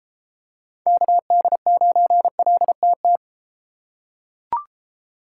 Morse code, heard as a keyed tone, about 32 wpm, 700 Hz. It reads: KD9LTT